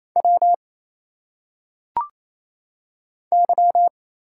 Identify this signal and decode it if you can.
Morse code, heard as a keyed tone, about 27 wpm, 700 Hz. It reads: W Y
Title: Single Letters - Phonetic Pronunciation 28wpm